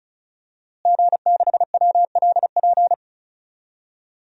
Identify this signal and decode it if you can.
Morse code, heard as a keyed tone, about 36 wpm, 700 Hz. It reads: G6WLP
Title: DXCC-100 Call Signs - 5 Characters - Repeated 2x - 35wpm